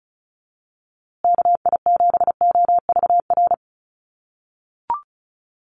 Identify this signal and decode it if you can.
Morse code, heard as a keyed tone, about 34 wpm, 700 Hz. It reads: KI7OVR
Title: USA Call Signs Repeated 3x 35wpm